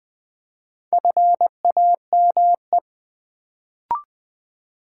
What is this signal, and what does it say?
Morse code, heard as a keyed tone, about 20 wpm, 700 Hz. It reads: FAME
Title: Build A Word - 1 Letter At A Time - 3 to 5-Letter Words - 20wpm